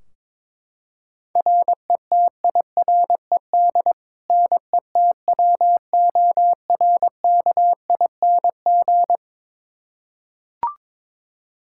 Morse code, heard as a keyed tone, about 22 wpm, 700 Hz. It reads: RETIRED NETWORKING
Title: K1USN SST QSO Practice 22wpm